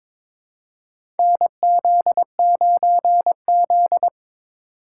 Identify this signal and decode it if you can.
Morse code, heard as a keyed tone, about 22 wpm, 700 Hz. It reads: NZ9Z
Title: USA Call Signs - 2x Character Spacing - 22wpm